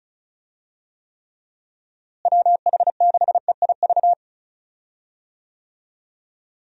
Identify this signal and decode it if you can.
Morse code, heard as a keyed tone, about 36 wpm, 700 Hz. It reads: WH6EIV
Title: USA Call Signs - 6 Characters - Repeated 3x - 35wpm